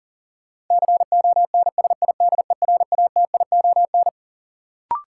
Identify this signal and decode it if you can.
Morse code, heard as a keyed tone, about 40 wpm, 700 Hz. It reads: CONSIDERATION
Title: Sentences from Top 1500 Words - New Word Drill - 40wpm